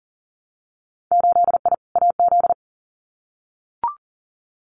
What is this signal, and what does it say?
Morse code, heard as a keyed tone, about 40 wpm, 700 Hz. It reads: 8I AZ
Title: Winter Field Day Exchanges Encore 40wpm